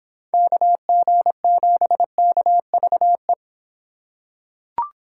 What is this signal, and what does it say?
Morse code, heard as a keyed tone, about 24 wpm, 700 Hz. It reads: KG7KVE